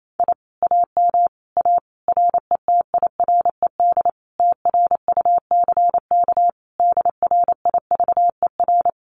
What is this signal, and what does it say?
Morse code, heard as a keyed tone, about 28 wpm, 700 Hz. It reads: I AM A RETIRED TRUCK DRIVER
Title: Common QSO Phrases Encore 28wpm